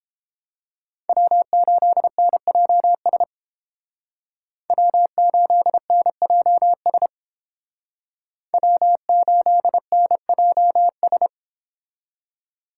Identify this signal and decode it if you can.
Morse code, heard as a keyed tone, about 29 wpm, 700 Hz. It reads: W8NJS W8NJS W8NJS